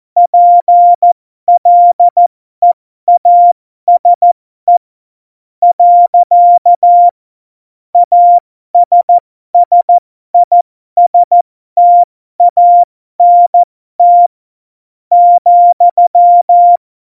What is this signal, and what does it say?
Morse code, heard as a keyed tone, about 14 wpm, spaced as for 11 wpm, 700 Hz. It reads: PLEASE. ASSISTANT,
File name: Morsekurs / Witze